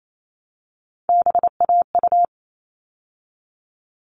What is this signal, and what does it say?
Morse code, heard as a keyed tone, about 28 wpm, 700 Hz. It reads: BAU
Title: All Cut Numbers - 3-Digit 28wpm